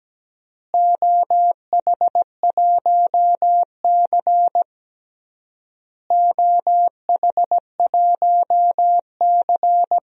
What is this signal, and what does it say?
Morse code, heard as a keyed tone, about 17 wpm, 700 Hz. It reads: OH1C OH1C
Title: DXCC-100 Call Signs - 4 Characters - Repeated 3x - 17wpm